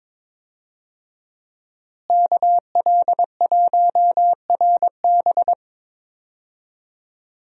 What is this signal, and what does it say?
Morse code, heard as a keyed tone, about 22 wpm, 700 Hz. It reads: KL1RB
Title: USA Call Signs - 5 Characters - Repeated 3x - 22wpm